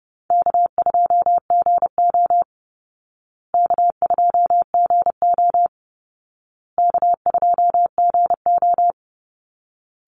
Morse code, heard as a keyed tone, about 30 wpm, 700 Hz. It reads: K2GO K2GO K2GO
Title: Call Signs Repeated 2x - 30wpm